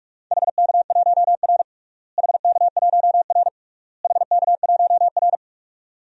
Morse code, heard as a keyed tone, about 46 wpm, 700 Hz. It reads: HK1R HK1R HK1R
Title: DXCC-100 Call Signs - 4 Characters - Repeated 3x - 45wpm